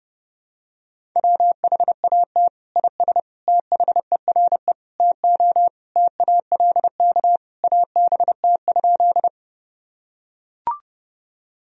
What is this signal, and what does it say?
Morse code, heard as a keyed tone, about 30 wpm, 700 Hz. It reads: WHAT IS THERE TO TALK ABT?
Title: Sentences from Top 400 Words and Abbreviations 30wpm